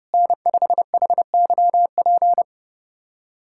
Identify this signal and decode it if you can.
Morse code, heard as a keyed tone, about 30 wpm, 700 Hz. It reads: N5HYP
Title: USA Callsign DE Callsign - 5 Characters - 30wpm